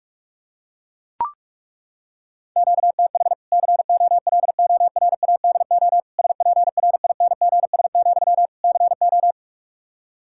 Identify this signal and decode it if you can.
Morse code, heard as a keyed tone, about 41 wpm, 700 Hz. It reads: QTH COLORADO SPRINGS, CO
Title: Practice QSO - Exchange 1 - Beginning - 45wpm